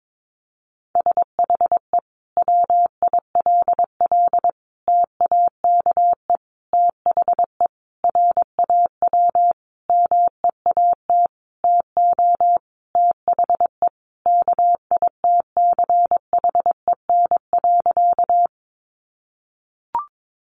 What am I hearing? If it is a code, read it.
Morse code, heard as a keyed tone, about 22 wpm, 700 Hz. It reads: SHE WILL TAKE THE RAW MEAT TO THE KITCHEN.
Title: Sentences from Top 1500 Words 22wpm